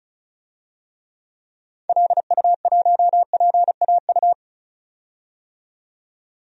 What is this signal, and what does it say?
Morse code, heard as a keyed tone, about 34 wpm, 700 Hz. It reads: LU1PAU
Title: DXCC-100 Call Signs - 6 Characters - 35wpm